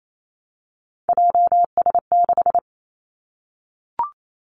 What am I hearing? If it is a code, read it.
Morse code, heard as a keyed tone, about 28 wpm, 700 Hz. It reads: JS6
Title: Three Challenging Letter-Number Combinations 28wpm